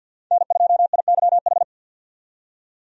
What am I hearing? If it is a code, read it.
Morse code, heard as a keyed tone, about 52 wpm, 700 Hz. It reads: N2IYH
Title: USA Call Signs - 5 Characters - 50wpm